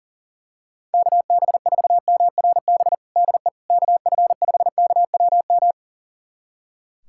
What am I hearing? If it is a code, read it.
Morse code, heard as a keyed tone, about 40 wpm, 700 Hz. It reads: KB4MRB DE KF5KWM
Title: USA Callsign DE Callsign 40wpm